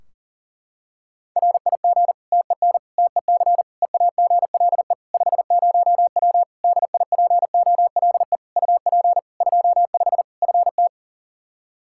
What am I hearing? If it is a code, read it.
Morse code, heard as a keyed tone, about 40 wpm, 700 Hz. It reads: RIG TEN TEC EAGLE 50W DIPOLE UP 25 FT
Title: Practice QSO - Exchange 2 - Info - 40wpm